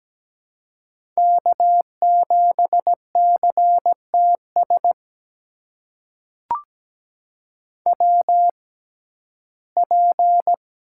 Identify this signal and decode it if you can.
Morse code, heard as a keyed tone, about 17 wpm, spaced as for 12 wpm, 700 Hz. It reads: K7CTS WP
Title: USA Call Signs - 5 Characters - 6x Character Spacing - 17wpm